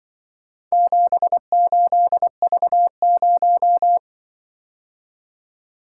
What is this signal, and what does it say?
Morse code, heard as a keyed tone, about 21 wpm, 700 Hz. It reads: 78V0
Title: Speed-Racing - Four Letter-Number Combinations - 20wpm